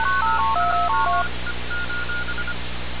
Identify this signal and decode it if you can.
Morse code, heard as a keyed tone, about 25 wpm, 1430 Hz, with noise. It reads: TIME:
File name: radiocap / REC z